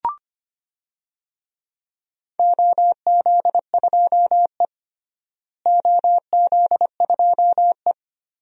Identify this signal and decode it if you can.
Morse code, heard as a keyed tone, about 25 wpm, 700 Hz. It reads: OZ2E OZ2E
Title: DXCC-100 Call Signs - 4 Characters - Repeated 3x - 25wpm